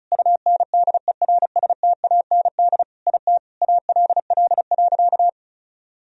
Morse code, heard as a keyed tone, about 36 wpm, 700 Hz. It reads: UNDERSTAND IT ALL.
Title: Sentences from Top 1000 Words - Review Part 4 - 35wpm